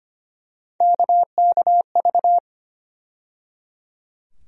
Morse code, heard as a keyed tone, about 25 wpm, 700 Hz. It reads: KKV
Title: USA Call Sign Postfixes - 3 Letters 25wpm